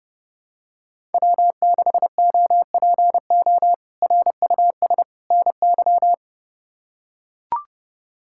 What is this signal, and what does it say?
Morse code, heard as a keyed tone, about 30 wpm, 700 Hz. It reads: W6OPO RUS NY
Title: K1USN SST QSO Practice - 30wpm